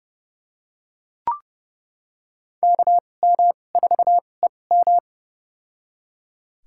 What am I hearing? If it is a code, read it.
Morse code, heard as a keyed tone, about 30 wpm, spaced as for 22 wpm, 690 Hz. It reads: KM4EM
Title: USA Call Signs - 5 Characters - 2x Character Spacing - 30wpm